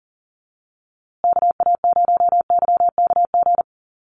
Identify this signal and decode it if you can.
Morse code, heard as a keyed tone, about 40 wpm, 700 Hz. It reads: KA0YKG
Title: USA Call Signs Repeated 3x 40wpm